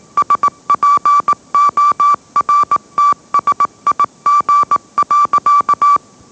Morse code, heard as a keyed tone, about 18 wpm, 1160 Hz. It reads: SPORTSIG.